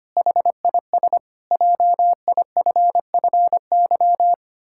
Morse code, heard as a keyed tone, about 25 wpm, 700 Hz. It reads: HIS JIFFY
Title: Pangram Challenge v2 - 25wpm